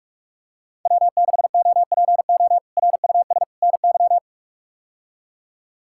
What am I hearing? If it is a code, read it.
Morse code, heard as a keyed tone, about 43 wpm, 700 Hz. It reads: W6OPO RUS NY
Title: K1USN SST QSO Practice - 45wpm